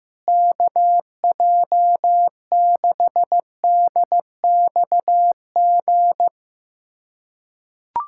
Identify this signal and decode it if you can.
Morse code, heard as a keyed tone, about 15 wpm, 700 Hz. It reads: KJ6DXG
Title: USA Callsign DE Callsign 15wpm